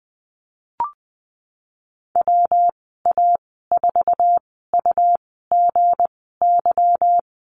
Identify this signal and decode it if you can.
Morse code, heard as a keyed tone, about 20 wpm, spaced as for 15 wpm, 700 Hz. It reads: WA4UGY
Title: 161 Lesson 26g - Letter g - Call Signs 20wpm